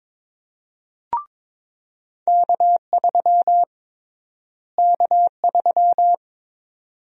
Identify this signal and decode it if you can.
Morse code, heard as a keyed tone, about 22 wpm, 700 Hz. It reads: K3 K3
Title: USA Call Sign Prefixes - Repeated 2x - 22wpm